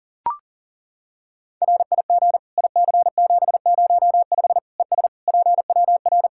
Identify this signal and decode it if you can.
Morse code, heard as a keyed tone, about 40 wpm, 700 Hz. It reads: RIG IC705 ES PWR
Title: Practice QSO - Exchange 2 - Info - 40wpm